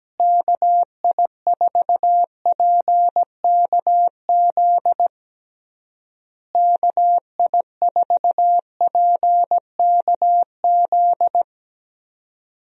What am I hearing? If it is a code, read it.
Morse code, heard as a keyed tone, about 17 wpm, 700 Hz. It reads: KI4PKZ KI4PKZ
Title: USA Call Signs - 6 Characters - Repeated 3x - 17wpm